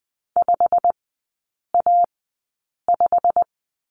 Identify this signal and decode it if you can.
Morse code, heard as a keyed tone, about 20 wpm, spaced as for 8 wpm, 690 Hz. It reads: 5A5
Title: 087 Lesson 15e - Number 5 - Three Characters 20wpm